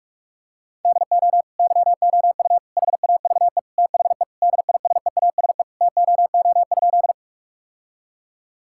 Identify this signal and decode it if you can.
Morse code, heard as a keyed tone, about 46 wpm, 700 Hz. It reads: DO YOU HAVE THE DISEASE TOO?
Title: Sentences from Top 1000 Words - Review Part 6 - 45wpm